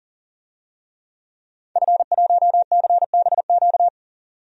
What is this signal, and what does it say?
Morse code, heard as a keyed tone, about 40 wpm, 700 Hz. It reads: F1CBQ
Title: DXCC-100 Call Signs - 5 Characters - 40wpm